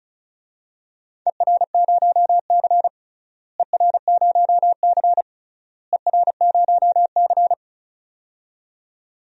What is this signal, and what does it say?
Morse code, heard as a keyed tone, about 34 wpm, 700 Hz. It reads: ER0C ER0C ER0C
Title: DXCC-100 Call Signs - 4 Characters - Repeated 3x - 35wpm